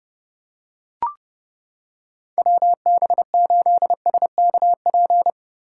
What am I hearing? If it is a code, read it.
Morse code, heard as a keyed tone, about 31 wpm, 700 Hz. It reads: WB8SKP
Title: K1USN SST QSO Practice - 30wpm